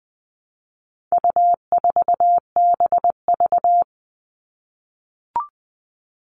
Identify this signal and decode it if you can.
Morse code, heard as a keyed tone, about 20 wpm, 700 Hz. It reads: U4BV
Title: All Cut Numbers - 4-Digit 20wpm